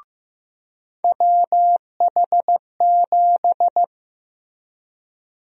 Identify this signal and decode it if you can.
Morse code, heard as a keyed tone, about 15 wpm, 700 Hz. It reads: WH7
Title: USA Call Sign Prefixes - Repeated 2x - 15wpm